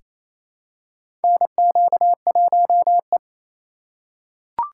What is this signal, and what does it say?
Morse code, heard as a keyed tone, about 28 wpm, 700 Hz. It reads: NQ1E
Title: USA Call Signs - 3x Character Spacing - 28wpm